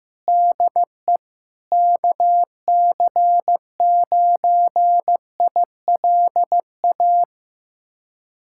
Morse code, heard as a keyed tone, about 15 wpm, 700 Hz. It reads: DE KC9ILA
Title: USA Callsign DE Callsign 15wpm